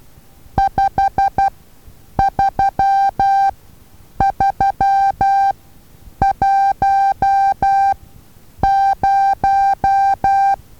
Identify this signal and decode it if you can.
Morse code, heard as a keyed tone, about 12 wpm, spaced as for 8 wpm, 790 Hz. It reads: 53310
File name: Taviro HA6VA A